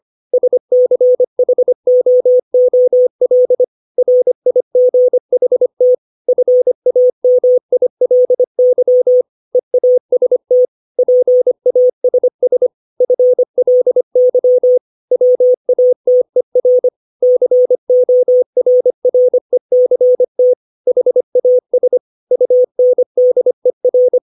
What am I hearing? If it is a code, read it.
Morse code, heard as a keyed tone, about 25 wpm, 500 Hz. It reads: SCHOOL RIGHT FAMILY EAST PASS FLY WATER CORRECT HAS UNDER
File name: 632 words / 632 Words 25WPM